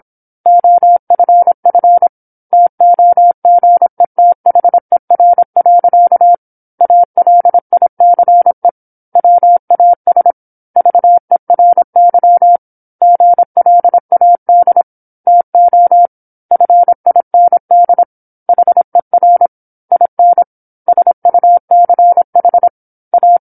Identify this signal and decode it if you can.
Morse code, heard as a keyed tone, about 25 wpm, 700 Hz. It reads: OFF TOGETHER. ALICE WAS VERY GLAD TO FIND HER IN SUCH A